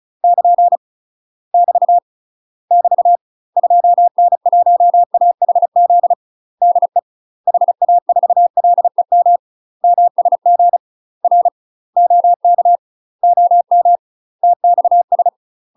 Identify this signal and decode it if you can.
Morse code, heard as a keyed tone, about 36 wpm, 700 Hz. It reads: <KN> <BT> <BT> 2N1AHZ DE HA4LEM MSG R OK OM TXS
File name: UK MASTER QSOs TRAINING/35WPM